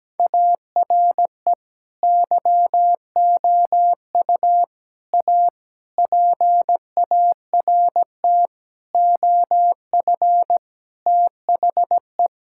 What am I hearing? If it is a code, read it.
Morse code, heard as a keyed tone, about 17 wpm, 700 Hz. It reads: ARE YOU A PART OF THE
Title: Sentences from Top 1600 Words 17wpm